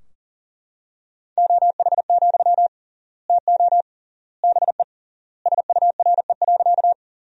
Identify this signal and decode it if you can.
Morse code, heard as a keyed tone, about 40 wpm, 700 Hz. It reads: OH, TO BE SURE.
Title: 3x Word-Spacing - Sentences from Top 500 Words - 40wpm